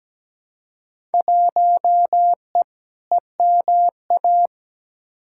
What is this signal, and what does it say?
Morse code, heard as a keyed tone, about 17 wpm, 700 Hz. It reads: 1E EMA
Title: Field Day Exchanges Encore - 17wpm